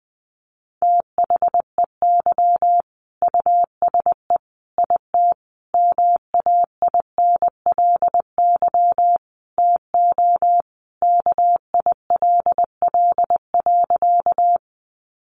Morse code, heard as a keyed tone, about 20 wpm, 700 Hz. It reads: THEY USE IT MAINLY TO KILL.
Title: Sentences from Top 1400 Words 20wpm